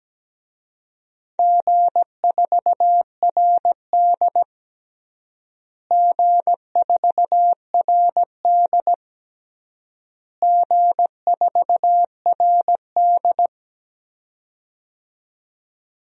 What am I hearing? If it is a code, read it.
Morse code, heard as a keyed tone, about 17 wpm, 700 Hz. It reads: G4RD G4RD G4RD
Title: DXCC-100 Call Signs - 4 Characters - Repeated 3x - 17wpm